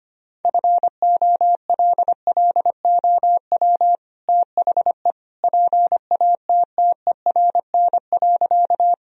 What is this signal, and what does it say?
Morse code, heard as a keyed tone, about 25 wpm, 700 Hz. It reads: FOLLOW THE PATTERN.
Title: Sentences from Top 1000 Words - Review Part 2 - 25wpm